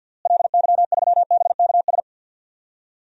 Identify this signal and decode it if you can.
Morse code, heard as a keyed tone, about 46 wpm, 700 Hz. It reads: LY3BKS
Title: DXCC-100 Call Signs - 6 Characters - Repeated 3x - 50wpm